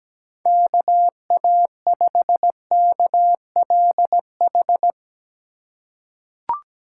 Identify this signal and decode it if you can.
Morse code, heard as a keyed tone, about 17 wpm, 700 Hz. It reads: KA5KLH
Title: USA Call Signs - 6 Characters - 2x Character Spacing - 17wpm